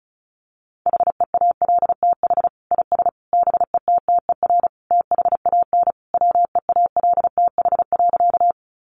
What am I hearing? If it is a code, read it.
Morse code, heard as a keyed tone, about 34 wpm, 700 Hz. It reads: HEALTH IS BETTER THAN WEALTH.
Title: Sentences from Top 1200 Words 35wpm